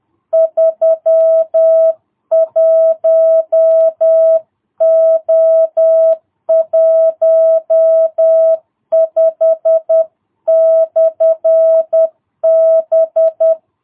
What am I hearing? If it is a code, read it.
Morse code, heard as a keyed tone, about 10 wpm, 650 Hz. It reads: 31O15/B
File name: jh8cbh / 2022cw